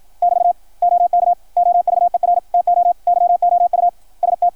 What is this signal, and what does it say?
Morse code, heard as a keyed tone, about 53 wpm, 690 Hz, with radio noise. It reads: <BT> OK OVER TO YOU ST
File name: archived / CRSnet